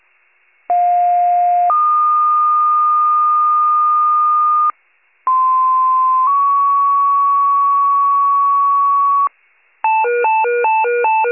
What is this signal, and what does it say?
Two-tone page, heard as a one-second tone, then a three-second tone, about 710 then 1220 Hz, about 1040 then 1120 Hz. It is heard from 0.7 to 9.3 s.
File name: wcalert